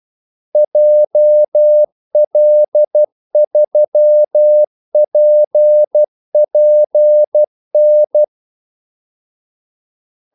Morse code, heard as a keyed tone, about 12 wpm, 600 Hz. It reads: JL3PPN
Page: 2 ekstra ordmellemrum mellem kaldesignalerne